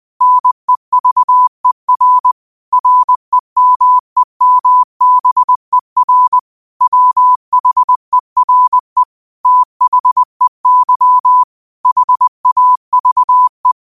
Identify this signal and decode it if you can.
Morse code, heard as a keyed tone, about 20 wpm, 1000 Hz. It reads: NEVER REMEMBER WHERE THEY HAVE
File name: qotdfiles / 20wpm